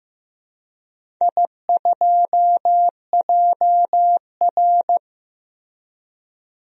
Morse code, heard as a keyed tone, about 15 wpm, 700 Hz. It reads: I2JR
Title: DXCC-100 Call Signs - 4 Characters - Repeated 2x - 15wpm